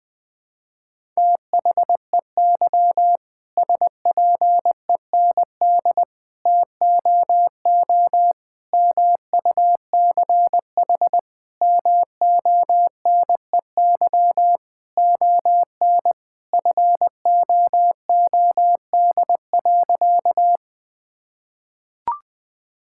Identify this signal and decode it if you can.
Morse code, heard as a keyed tone, about 20 wpm, 700 Hz. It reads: THEY SPEND TOO MUCH MONEY ON FOOD.
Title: Sentences from Top 1000 Words - Review Part 5 - 20wpm